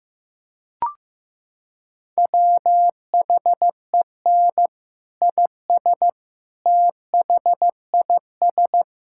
Morse code, heard as a keyed tone, about 15 wpm, 700 Hz. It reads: WHEN IS THIS
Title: Sentences from Top 1600 Words 15wpm